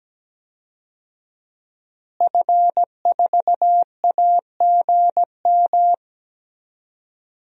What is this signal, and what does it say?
Morse code, heard as a keyed tone, about 17 wpm, 700 Hz. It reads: F4AGM
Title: DXCC-100 Call Signs - 5 Characters - Repeated 3x - 17wpm